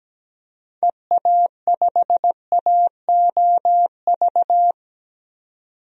Morse code, heard as a keyed tone, about 17 wpm, 700 Hz. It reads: EA5AOV
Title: DXCC-100 Call Signs - 6 Characters - 17wpm